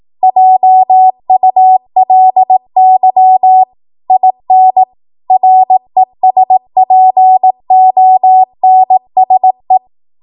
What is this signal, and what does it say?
Morse code, heard as a keyed tone, about 18 wpm, 750 Hz. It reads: JULY IN RESPONSE
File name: Morse / 18WPM